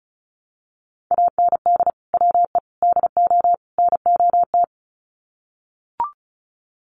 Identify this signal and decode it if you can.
Morse code, heard as a keyed tone, about 34 wpm, 700 Hz. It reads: AND WE DO NOT
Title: Sets of 4 Words 35wpm